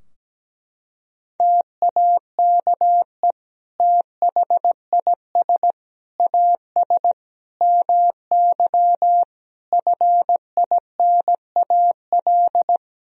Morse code, heard as a keyed tone, about 17 wpm, 700 Hz. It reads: TAKE THIS AS MY FINAL
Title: Sentences from Top 1000 Words - Review Part 2 - 17wpm